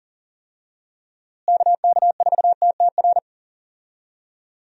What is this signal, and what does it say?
Morse code, heard as a keyed tone, about 40 wpm, 700 Hz. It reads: KK4TTR
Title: USA Call Signs - 6 Characters - Repeated 2x - 40wpm